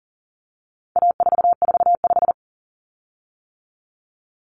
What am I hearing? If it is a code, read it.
Morse code, heard as a keyed tone, about 40 wpm, 700 Hz. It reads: A445
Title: Common Cut Numbers - 4-Digit 40wpm